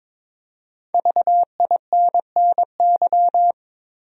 Morse code, heard as a keyed tone, about 22 wpm, 700 Hz. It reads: VINNY
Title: USA Common Names - Max 5 Letters - 22wpm